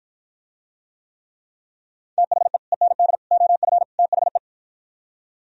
Morse code, heard as a keyed tone, about 52 wpm, 700 Hz. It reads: THE END OF THE